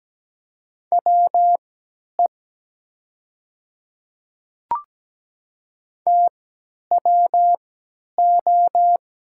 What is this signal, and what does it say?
Morse code, heard as a keyed tone, about 17 wpm, spaced as for 10 wpm, 700 Hz. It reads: WE TWO
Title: 3x Character Spacing - Top 200 Words - 17wpm